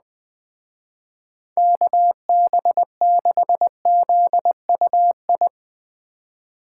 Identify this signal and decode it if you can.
Morse code, heard as a keyed tone, about 20 wpm, 700 Hz. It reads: KB6ZUI
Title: Speed Racing - USA Call Signs 17wpm